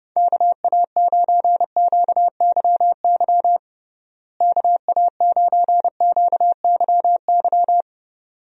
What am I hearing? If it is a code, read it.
Morse code, heard as a keyed tone, about 32 wpm, 700 Hz. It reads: KA9QYY KA9QYY
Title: USA Call Signs - 6 Characters - Repeated 2x - 30wpm